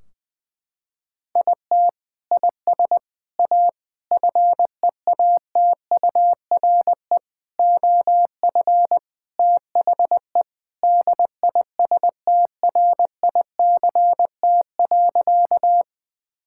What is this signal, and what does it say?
Morse code, heard as a keyed tone, about 20 wpm, 700 Hz. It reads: IT IS A FEATURE OF THE DISTRICT.
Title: Sentences from Top 1600 Words 20wpm